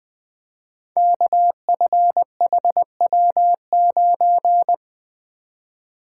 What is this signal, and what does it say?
Morse code, heard as a keyed tone, about 20 wpm, 700 Hz. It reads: KFHW9
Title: Code Groups - 5 Letter:Numbers - 20wpm